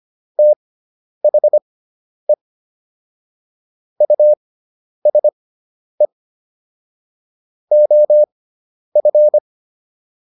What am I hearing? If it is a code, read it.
Morse code, heard as a keyed tone, about 25 wpm, spaced as for 10 wpm, 600 Hz. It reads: THE USE OF